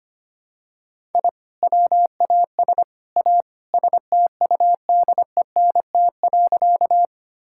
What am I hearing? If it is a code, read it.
Morse code, heard as a keyed tone, about 25 wpm, 700 Hz. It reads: I WAS A STUDENT.
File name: practice / 25wpm / Sentences with 3-Word Phrases - 25wpm